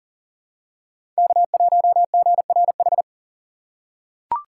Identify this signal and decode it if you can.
Morse code, heard as a keyed tone, about 40 wpm, 700 Hz. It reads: K1GRH
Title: USA Call Signs - 5 Characters - 2x Character Spacing - 40wpm